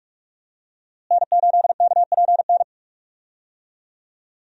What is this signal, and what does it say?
Morse code, heard as a keyed tone, about 42 wpm, 700 Hz. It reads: N8KPN
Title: USA Call Signs - 5 Characters - Repeated 3x - 45wpm